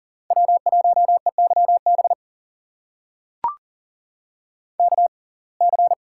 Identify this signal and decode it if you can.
Morse code, heard as a keyed tone, about 40 wpm, spaced as for 29 wpm, 700 Hz. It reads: W1EYB KC
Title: USA Call Signs - 5 Characters - 6x Character Spacing - 40wpm